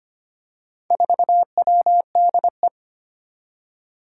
Morse code, heard as a keyed tone, about 25 wpm, 700 Hz. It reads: 4WDE
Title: Four Challenging Letter-Number Combinations 25wpm